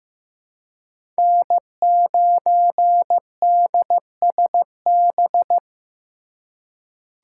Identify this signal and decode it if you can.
Morse code, heard as a keyed tone, about 15 wpm, 700 Hz. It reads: N9DSB
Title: USA Call Signs - 4x Character Spacing - 15wpm